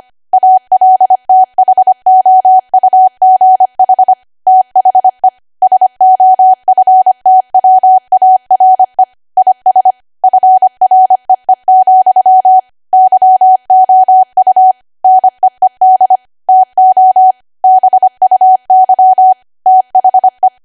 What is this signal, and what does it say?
Morse code, heard as a keyed tone, about 25 wpm, 750 Hz. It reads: ALTHOUGH THE SOFTWARE IS FREE, YOU NEED TO BUY THE
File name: Archive / 25 WPM / 25WPM